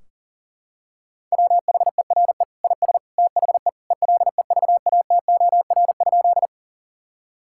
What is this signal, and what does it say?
Morse code, heard as a keyed tone, about 40 wpm, 700 Hz. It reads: WHERE IS THE ELEVATOR?
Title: Sentences from Top 1600 Words 40wpm